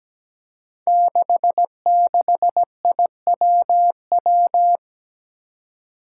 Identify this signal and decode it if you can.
Morse code, heard as a keyed tone, about 17 wpm, 700 Hz. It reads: 66IWW
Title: Five Challenging Letter-Number Combinations 17wpm